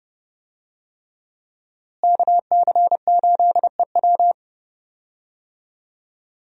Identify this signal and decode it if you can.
Morse code, heard as a keyed tone, about 30 wpm, 700 Hz. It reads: KC8EW
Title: USA Call Signs - 5 Characters - Repeated 3x - 30wpm